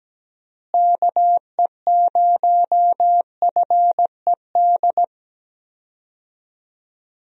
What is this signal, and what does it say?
Morse code, heard as a keyed tone, about 17 wpm, 700 Hz. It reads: KE0FED
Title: USA Call Signs - 6 Characters - Repeated 3x - 17wpm